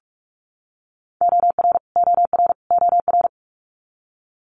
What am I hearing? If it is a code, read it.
Morse code, heard as a keyed tone, about 46 wpm, 700 Hz. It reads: OR OR OR
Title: Multi-letter Phonemes Repeated 3x 45wpm